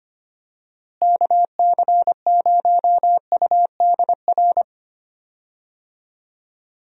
Morse code, heard as a keyed tone, about 25 wpm, 700 Hz. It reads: KC0UDR
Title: USA Call Signs - 6 Characters - Repeated 2x - 25wpm